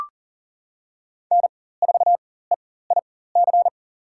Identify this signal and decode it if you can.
Morse code, heard as a keyed tone, about 37 wpm, spaced as for 18 wpm, 700 Hz. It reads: N4EIC
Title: USA Call Signs - 5 Characters - 4x Character Spacing - 40wpm